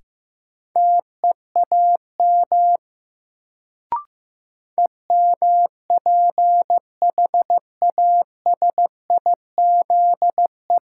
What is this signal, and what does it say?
Morse code, heard as a keyed tone, about 15 wpm, 700 Hz. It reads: TEAM EMPHASIZE
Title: Sentences from Top 1500 Words - New Word Drill - 15wpm